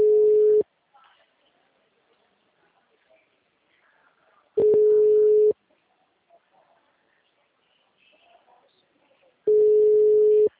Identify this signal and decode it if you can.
Ringback tone (Europe, 425 Hz), with speech.